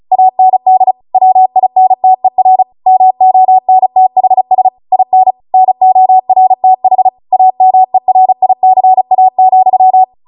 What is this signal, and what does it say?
Morse code, heard as a keyed tone, about 36 wpm, 750 Hz. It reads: AND WINTER MONTHS IN NORTH AMERICA,